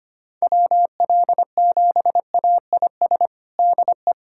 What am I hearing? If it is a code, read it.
Morse code, heard as a keyed tone, about 25 wpm, 700 Hz. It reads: WL7AIS DE
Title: USA Callsign DE Callsign 25wpm